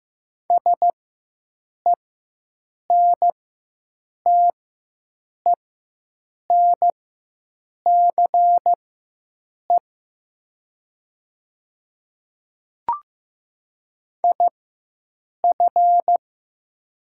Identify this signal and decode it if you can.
Morse code, heard as a keyed tone, about 15 wpm, spaced as for 7 wpm, 700 Hz. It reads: SENTENCE IF
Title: 4x Character Spacing - Top 300 Words - 15wpm